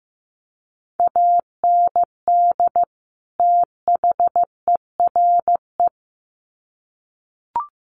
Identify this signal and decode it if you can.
Morse code, heard as a keyed tone, about 15 wpm, 700 Hz. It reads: AND THERE
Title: Sets of 2 Words 15wpm